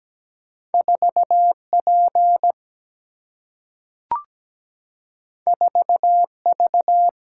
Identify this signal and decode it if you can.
Morse code, heard as a keyed tone, about 17 wpm, 700 Hz. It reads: 4 P 4 V
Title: Single-Letter Single-Number 17wpm